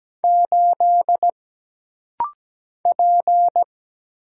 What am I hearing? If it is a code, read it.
Morse code, heard as a keyed tone, about 17 wpm, 700 Hz. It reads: 8 P
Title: Single Letter-Number Alt - Rapid-Fire 17wpm